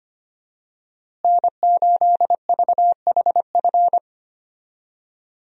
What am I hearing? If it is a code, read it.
Morse code, heard as a keyed tone, about 25 wpm, 700 Hz. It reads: N8VHF
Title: USA Call Signs - 5 Characters - Repeated 2x - 25wpm